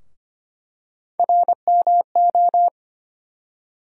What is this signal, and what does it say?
Morse code, heard as a keyed tone, about 25 wpm, 700 Hz. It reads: RMO
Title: USA Call Sign Postfixes - 3 Letters 25wpm